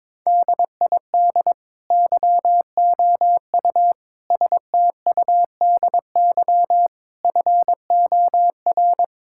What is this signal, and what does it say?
Morse code, heard as a keyed tone, about 22 wpm, 700 Hz. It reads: DID YOU STUDY FOR
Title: Sentences from Top 1000 Words - Review Part 5 - 22wpm